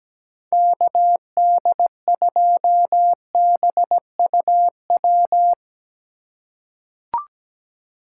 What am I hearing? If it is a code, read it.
Morse code, heard as a keyed tone, about 17 wpm, 700 Hz. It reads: KD2BUW
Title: USA Call Signs - 6 Characters - 4x Character Spacing - 17wpm